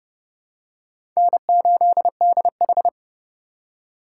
Morse code, heard as a keyed tone, about 30 wpm, 700 Hz. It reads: N8DH
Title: USA Call Signs - 4 Characters - 4x Character Spacing - 30wpm